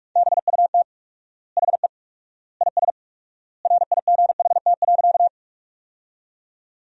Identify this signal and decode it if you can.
Morse code, heard as a keyed tone, about 42 wpm, 700 Hz. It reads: BUT HE IS RIGHT.
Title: 4x Word-Spacing - Sentences from Top 200 Words - 45wpm